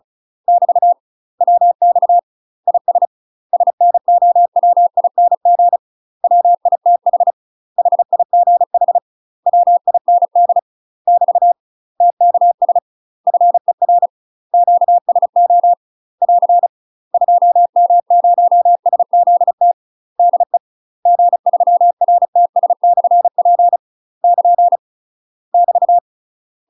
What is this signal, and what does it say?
Morse code, heard as a keyed tone, about 34 wpm, 700 Hz. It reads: <BT> WX IS SNOWING WITH HIGH WIND <BT> TKS FER QSO <AR> 2M0SZT DE G3RTS/P <KN> <BT>
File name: UK MASTER QSOs TRAINING / 35WPM / UK MASTER QSOs 0040